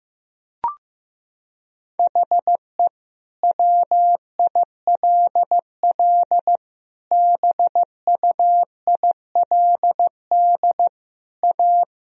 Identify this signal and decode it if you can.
Morse code, heard as a keyed tone, about 15 wpm, 700 Hz. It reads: HE WILL BUILD A
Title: Sentences from Top 1000 Words - Review Part 1 - 15wpm